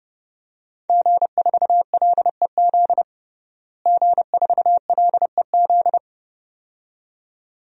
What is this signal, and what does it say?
Morse code, heard as a keyed tone, about 30 wpm, 700 Hz. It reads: G4LEZ G4LEZ
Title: DXCC-100 Call Signs - 5 Characters - Repeated 2x - 30wpm